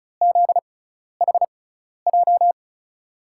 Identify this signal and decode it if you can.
Morse code, heard as a keyed tone, about 34 wpm, spaced as for 12 wpm, 700 Hz. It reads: ZHJ
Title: USA Call Signs - 6 Characters - 6x Character Spacing - 35wpm